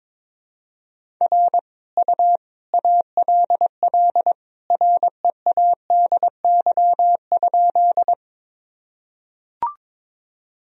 Morse code, heard as a keyed tone, about 22 wpm, 700 Hz. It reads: R U ALL READY?
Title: Sentences from Top 400 Words and Abbreviations 22wpm